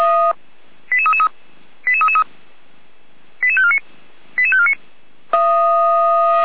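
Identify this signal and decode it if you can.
Five-tone selective call (ZVEI), heard as quick runs of five tones, tones of about 70 ms. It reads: 88333 88333 88549 88549
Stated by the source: ZVEI five-tone selective call received on SDR.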